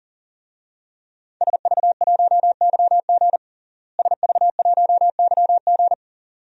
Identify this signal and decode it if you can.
Morse code, heard as a keyed tone, about 40 wpm, 690 Hz. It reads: SV1YG SV1YG
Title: DXCC-100 Call Signs - 5 Characters - Repeated 2x - 40wpm